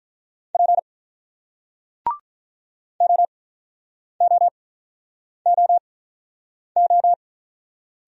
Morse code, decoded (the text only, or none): P OOOO